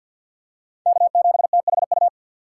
Morse code, heard as a keyed tone, about 52 wpm, 700 Hz. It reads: K7THU
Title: USA Call Signs - 5 Characters - 2x Character Spacing - 50wpm